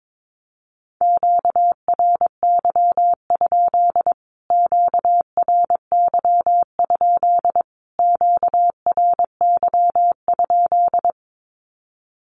Morse code, heard as a keyed tone, about 22 wpm, 700 Hz. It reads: QRY? QRY? QRY?
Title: Q-Codes Repeated 3x 22wpm